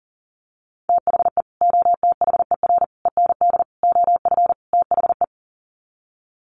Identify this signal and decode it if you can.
Morse code, heard as a keyed tone, about 40 wpm, 700 Hz. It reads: THE OTHER END OF THE
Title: Sets of 5 Words 40wpm